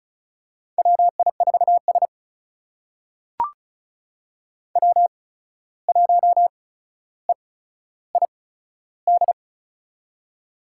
Morse code, heard as a keyed tone, about 36 wpm, spaced as for 14 wpm, 700 Hz. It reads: WI4S W1EID
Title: USA Call Signs - 8x Character Spacing - 35wpm